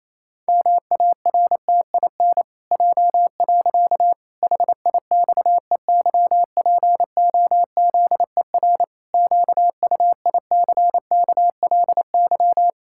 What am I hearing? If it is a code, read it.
Morse code, heard as a keyed tone, about 27 wpm, 700 Hz. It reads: MARTIN J. HIXEYPOZER QUICKLY
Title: Pangram Challenge v2 - 28wpm